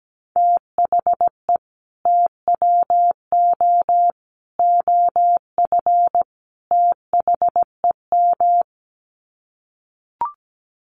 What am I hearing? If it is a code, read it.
Morse code, heard as a keyed tone, about 17 wpm, 700 Hz. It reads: THE TWO OF THEM
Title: Sets of 4 Words 17wpm